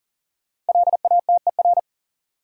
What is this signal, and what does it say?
Morse code, heard as a keyed tone, about 40 wpm, 700 Hz. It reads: LATER
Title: Sentences from Top 700 Words - New Word Drill - 40wpm